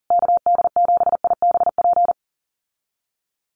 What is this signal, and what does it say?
Morse code, heard as a keyed tone, about 40 wpm, 700 Hz. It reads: KD7IBP
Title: USA Call Signs Repeated 3x 40wpm